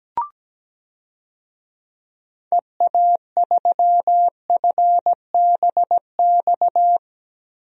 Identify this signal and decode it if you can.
Morse code, heard as a keyed tone, about 17 wpm, 700 Hz. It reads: EA3FBX
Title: DXCC-100 Call Signs - 6 Characters - Repeated 2x - 17wpm